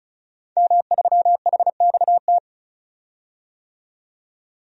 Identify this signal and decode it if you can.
Morse code, heard as a keyed tone, about 34 wpm, 700 Hz. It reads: M3HXT
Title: DXCC-100 Call Signs - 5 Characters - 35wpm